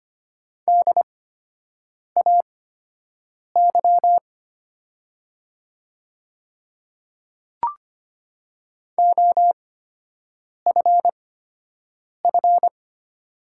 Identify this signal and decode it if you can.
Morse code, heard as a keyed tone, about 25 wpm, spaced as for 7 wpm, 700 Hz. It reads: DAY OFF
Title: 8x Character Spacing - Top 300 Words - 25wpm